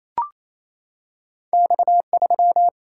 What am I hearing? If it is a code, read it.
Morse code, heard as a keyed tone, about 27 wpm, 700 Hz. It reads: X 3
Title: Single-Letter Single-Number 28wpm